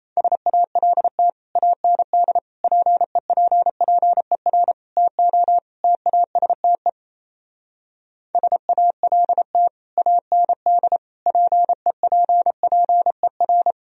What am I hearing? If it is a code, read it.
Morse code, heard as a keyed tone, about 34 wpm, 700 Hz. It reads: SALT AND PEPPER TO TASTE SALT AND PEPPER
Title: Speed-Racing - Sets of 5 Words 28wpm